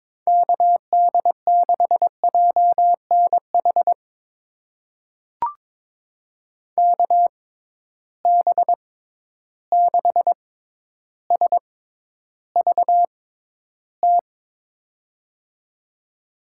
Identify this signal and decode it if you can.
Morse code, heard as a keyed tone, about 22 wpm, spaced as for 11 wpm, 700 Hz. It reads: KD6JNH KB6SVT
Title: USA Call Signs - 6 Characters - 6x Character Spacing - 22wpm